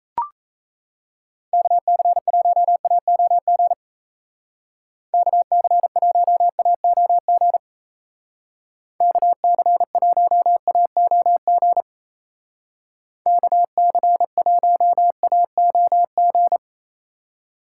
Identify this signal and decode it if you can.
Morse code, heard as a keyed tone, about 35 wpm, 700 Hz. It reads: KC1AOG KC1AOG KC1AOG KC1AOG
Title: Speed Racing - USA 2x3 Call Signs 28wpm